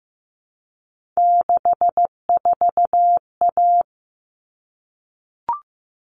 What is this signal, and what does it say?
Morse code, heard as a keyed tone, about 15 wpm, 700 Hz. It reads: 64A
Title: All Cut Numbers - 3-Digit 15wpm